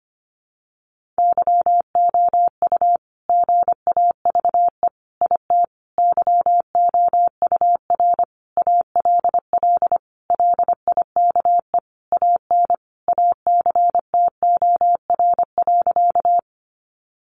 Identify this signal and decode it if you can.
Morse code, heard as a keyed tone, about 25 wpm, 700 Hz. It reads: YOU GAVE IT YOUR ALL LIKE AN ACTOR.
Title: Sentences from Top 1500 Words 25wpm